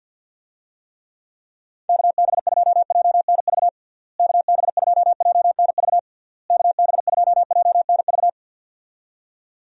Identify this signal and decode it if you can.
Morse code, heard as a keyed tone, about 52 wpm, 700 Hz. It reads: KB2JNV KB2JNV KB2JNV
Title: USA Call Signs - 6 Characters - Repeated 3x - 50wpm